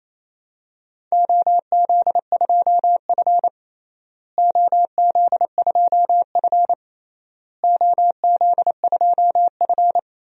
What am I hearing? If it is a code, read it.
Morse code, heard as a keyed tone, about 28 wpm, 700 Hz. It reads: OZ2F OZ2F OZ2F
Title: DXCC-100 Call Signs - 4 Characters - Repeated 3x - 28wpm